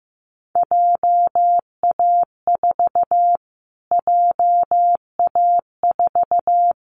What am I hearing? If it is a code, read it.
Morse code, heard as a keyed tone, about 15 wpm, 700 Hz. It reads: JA4 JA4
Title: Callsign Prefixes for Top 25 DXCC - Repeated 3x - 15wpm